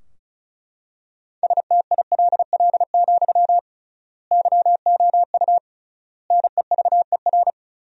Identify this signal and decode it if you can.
Morse code, heard as a keyed tone, about 36 wpm, 700 Hz. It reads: STILL, YOU NEVER
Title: 3x Word-Spacing - Sentences from Top 300 Words - 35wpm